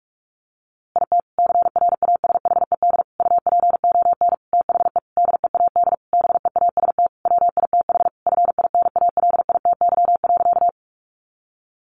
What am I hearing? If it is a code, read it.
Morse code, heard as a keyed tone, about 44 wpm, 700 Hz. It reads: IT CRASHED UPON THE DEAD BEAST WITH FINALITY.
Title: A Sound Of Thunder 45wpm